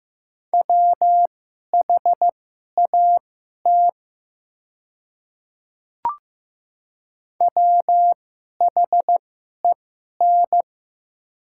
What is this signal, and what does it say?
Morse code, heard as a keyed tone, about 15 wpm, spaced as for 11 wpm, 700 Hz. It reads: WHAT WHEN
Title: 2x Character Spacing - Top 100 Words - 15wpm